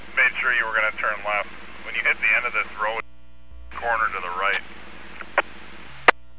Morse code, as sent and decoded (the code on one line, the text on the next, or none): none